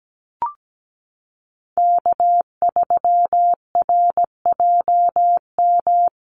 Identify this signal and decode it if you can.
Morse code, heard as a keyed tone, about 17 wpm, 700 Hz. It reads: K3RJM
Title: USA Call Signs Repeated 2x 17wpm